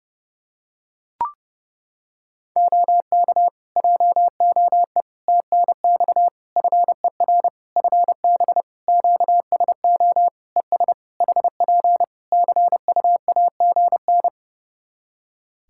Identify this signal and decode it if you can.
Morse code, heard as a keyed tone, about 30 wpm, 700 Hz. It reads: OK JOE TNX FER FB QSO ES HP CUAGN
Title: K1USN SST QSO Practice 30wpm